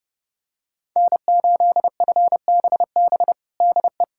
Morse code, heard as a keyed tone, about 30 wpm, 700 Hz. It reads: N8FBB DE
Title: USA Callsign DE Callsign - 5 Characters - 30wpm